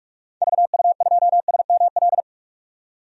Morse code, heard as a keyed tone, about 43 wpm, 700 Hz. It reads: VU2SML
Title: DXCC-100 Call Signs - 6 Characters - 45wpm